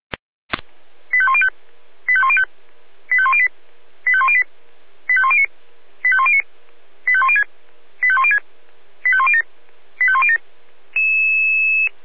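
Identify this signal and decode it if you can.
Five-tone selective call (ZVEI), heard as quick runs of five tones, tones of about 70 ms. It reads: 85105 85105 85108 85108 85109 85109 85106 85106 85107 85107
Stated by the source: ZVEI five-tone selective call received by SDR.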